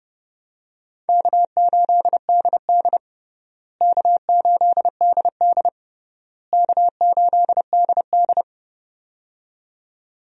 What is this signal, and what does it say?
Morse code, heard as a keyed tone, about 30 wpm, 700 Hz. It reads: K8DD K8DD K8DD
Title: USA Call Signs - 4 Characters - Repeated 3x - 30wpm